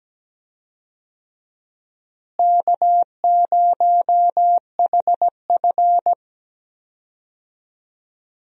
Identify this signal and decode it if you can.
Morse code, heard as a keyed tone, about 17 wpm, 700 Hz. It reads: K0HF
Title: USA Call Signs - 4 Characters - Repeated 2x - 17wpm